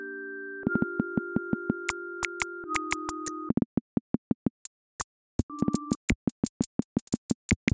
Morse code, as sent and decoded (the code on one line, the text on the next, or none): none